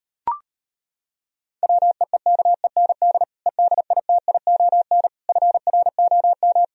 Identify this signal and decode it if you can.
Morse code, heard as a keyed tone, about 33 wpm, 700 Hz. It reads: WEEKEND EDITION FROM